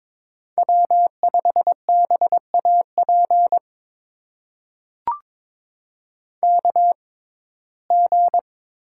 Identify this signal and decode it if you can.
Morse code, heard as a keyed tone, about 21 wpm, spaced as for 16 wpm, 700 Hz. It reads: W5BAP KG
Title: USA Call Signs - 6x Character Spacing - 22wpm